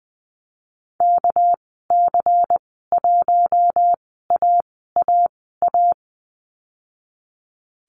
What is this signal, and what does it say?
Morse code, heard as a keyed tone, about 19 wpm, spaced as for 14 wpm, 700 Hz. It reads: KC1AAA
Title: 213 Lesson 34g - Number 8 - Call Signs 20wpm